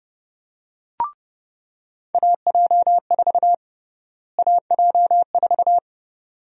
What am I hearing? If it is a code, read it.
Morse code, heard as a keyed tone, about 30 wpm, 700 Hz. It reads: AJ4 AJ4
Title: USA Call Sign Prefixes - Repeated 2x - 30wpm